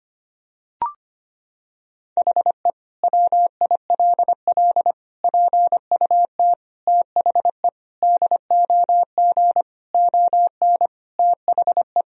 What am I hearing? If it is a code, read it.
Morse code, heard as a keyed tone, about 25 wpm, 700 Hz. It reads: HE WILL PUT THE DOG ON THE
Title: Sentences from Top 1000 Words - Review Part 2 - 25wpm